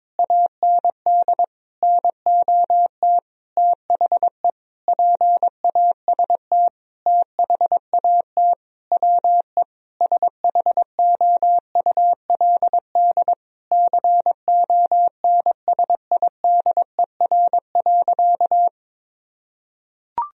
Morse code, heard as a keyed tone, about 21 wpm, 700 Hz. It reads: AND NOT THE PAST THAT WE SHOULD CONSIDER.
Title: Sentences from Top 1000 Words - Review Part 4 - 22wpm